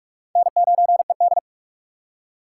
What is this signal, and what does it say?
Morse code, heard as a keyed tone, about 42 wpm, 700 Hz. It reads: N9ED
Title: USA Call Signs - 4 Characters - 3x Character Spacing - 45wpm